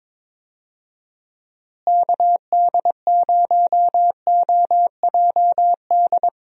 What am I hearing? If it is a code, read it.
Morse code, heard as a keyed tone, about 22 wpm, 700 Hz. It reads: KD0OJD
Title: USA Call Signs - 6 Characters - Repeated 2x - 22wpm